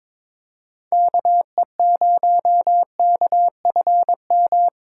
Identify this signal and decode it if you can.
Morse code, heard as a keyed tone, about 22 wpm, 700 Hz. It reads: KE0KFM
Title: USA Call Signs - 6 Characters - 4x Character Spacing - 22wpm